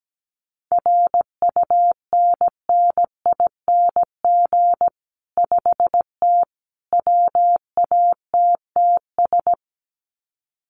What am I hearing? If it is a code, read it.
Morse code, heard as a keyed tone, about 17 wpm, 700 Hz. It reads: RUNNING 5T WATTS
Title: Common QSO Phrases 17wpm